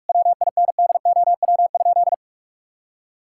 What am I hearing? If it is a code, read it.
Morse code, heard as a keyed tone, about 46 wpm, 700 Hz. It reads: WINDOW?
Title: Sentences from Top 1600 Words 45wpm